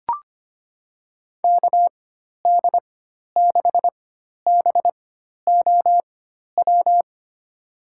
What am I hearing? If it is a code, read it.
Morse code, heard as a keyed tone, about 25 wpm, spaced as for 12 wpm, 700 Hz. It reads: KD6BOW
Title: USA Call Signs - 4x Character Spacing - 25wpm